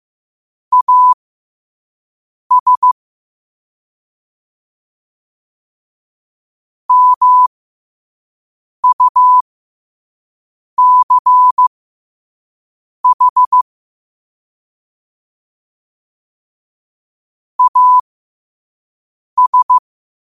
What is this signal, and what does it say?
Morse code, heard as a keyed tone, about 15 wpm, spaced as for 5 wpm, 1000 Hz. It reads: AS MUCH AS
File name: qotdfiles / 5wpm